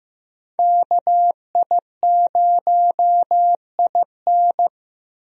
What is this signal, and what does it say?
Morse code, heard as a keyed tone, about 15 wpm, 700 Hz. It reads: KI0IN
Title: USA Call Signs - 5 Characters - Repeated 2x - 15wpm